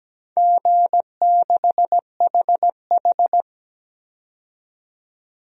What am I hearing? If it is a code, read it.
Morse code, heard as a keyed tone, about 17 wpm, 700 Hz. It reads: G6HH
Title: DXCC-100 Call Signs - 4 Characters - 17wpm